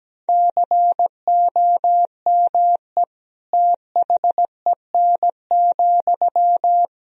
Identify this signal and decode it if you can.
Morse code, heard as a keyed tone, about 17 wpm, 700 Hz. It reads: COME THEN,
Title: Sentences from Top 1000 Words - Review Part 2 - 17wpm